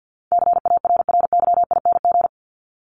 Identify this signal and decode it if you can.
Morse code, heard as a keyed tone, about 51 wpm, 700 Hz. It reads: CARRYING
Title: ING Suffix - Top 100 Words with Base Words 50wpm